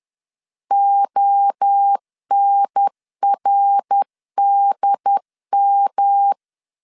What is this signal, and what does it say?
Morse code, heard as a keyed tone, about 10 wpm, 800 Hz. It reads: ONRDM
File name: CW / DET / 1 3 1500C S C 10wpm 3335